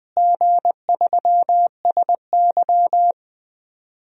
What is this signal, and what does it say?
Morse code, heard as a keyed tone, about 21 wpm, 700 Hz. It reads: G3SY
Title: DXCC-100 Call Signs - 4 Characters - Repeated 3x - 20wpm